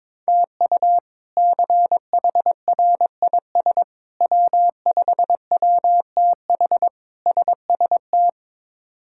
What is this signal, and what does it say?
Morse code, heard as a keyed tone, about 22 wpm, 700 Hz. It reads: TU CHRIS W5WTH SST
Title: K1USN SST QSO Practice - 22wpm